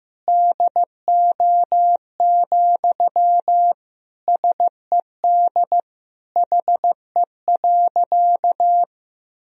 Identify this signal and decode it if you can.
Morse code, heard as a keyed tone, about 15 wpm, 700 Hz. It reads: DO, SED HE.
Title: Sentences from Top 100 Words and Abbreviations 15wpm